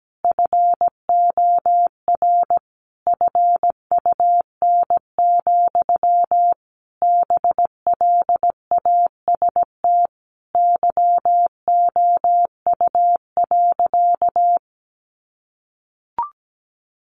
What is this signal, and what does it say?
Morse code, heard as a keyed tone, about 17 wpm, 700 Hz. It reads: FOR FUN, BLAST YOU.
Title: A Sound Of Thunder 17wpm